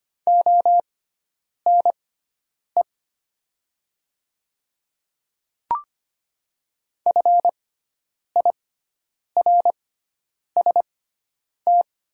Morse code, heard as a keyed tone, about 25 wpm, spaced as for 9 wpm, 700 Hz. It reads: ONE FIRST
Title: 6x Character Spacing - Top 100 Words - 25wpm